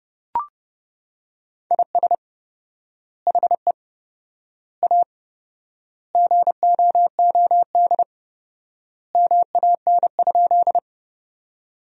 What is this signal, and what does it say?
Morse code, heard as a keyed tone, about 30 wpm, 700 Hz. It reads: IS HE A GOOD MAN?
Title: 4x Word-Spacing - Sentences from Top 200 Words - 30wpm